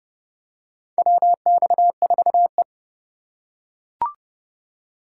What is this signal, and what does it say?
Morse code, heard as a keyed tone, about 29 wpm, 700 Hz. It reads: WX4E
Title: USA Call Signs - 4 Characters - 4x Character Spacing - 30wpm